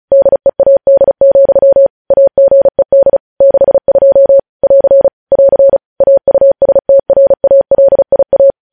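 Morse code, heard as a keyed tone, about 36 wpm, 550 Hz. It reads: DEAD, AGED 62 <AR> <AR> AUSTRALIA
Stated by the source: News headlines sent in Morse code at 35 WPM.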